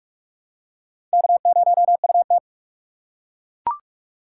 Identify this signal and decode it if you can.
Morse code, heard as a keyed tone, about 42 wpm, 700 Hz. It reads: K0UT